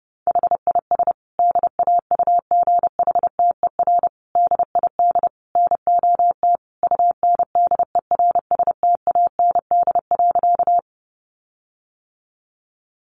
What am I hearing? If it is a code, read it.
Morse code, heard as a keyed tone, about 30 wpm, 700 Hz. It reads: HIS DAUGHTER DID NOT UNDERSTAND.
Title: Sentences from Top 1500 Words 30wpm